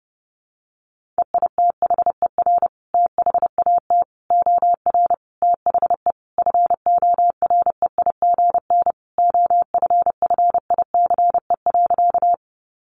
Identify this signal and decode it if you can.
Morse code, heard as a keyed tone, about 30 wpm, 700 Hz. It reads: EITHER THAT OR THE FOREIGN OFFICE.
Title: Sentences from Top 1400 Words 30wpm